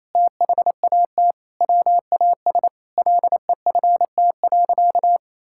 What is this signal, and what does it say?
Morse code, heard as a keyed tone, about 28 wpm, 700 Hz. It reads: THAT WAS LEFT.
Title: Sentences from Top 1600 Words 28wpm